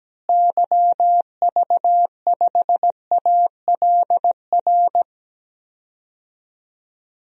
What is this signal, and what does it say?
Morse code, heard as a keyed tone, about 17 wpm, 700 Hz. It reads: YV5ALR
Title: DXCC-100 Call Signs - 6 Characters - Repeated 2x - 17wpm